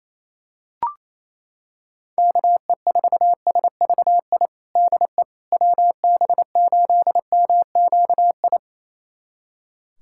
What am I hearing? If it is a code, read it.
Morse code, heard as a keyed tone, about 27 wpm, 700 Hz. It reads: KE4SVI DE WB8MQI
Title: USA Callsign DE Callsign - 6 Characters - 28wpm